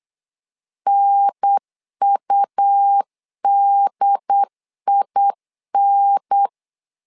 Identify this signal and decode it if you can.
Morse code, heard as a keyed tone, about 8 wpm, 800 Hz. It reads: NUDIN
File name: LEZIONI / 3 MWRO / 3 MW 8wpm 2501